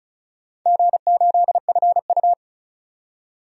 Morse code, heard as a keyed tone, about 36 wpm, 700 Hz. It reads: G8FU
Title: DXCC-100 Call Signs - 4 Characters - 35wpm